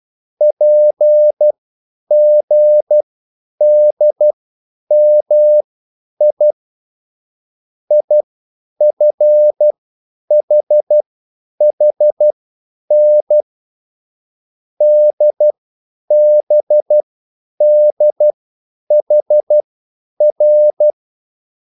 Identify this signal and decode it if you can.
Morse code, heard as a keyed tone, about 12 wpm, spaced as for 9 wpm, 600 Hz. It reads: PGDMI IFHHN DBDHR